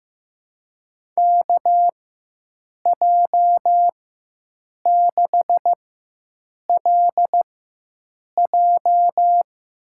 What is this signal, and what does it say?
Morse code, heard as a keyed tone, about 15 wpm, spaced as for 7 wpm, 700 Hz. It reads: KJ6LJ
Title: USA Call Signs - 6 Characters - 4x Character Spacing - 15wpm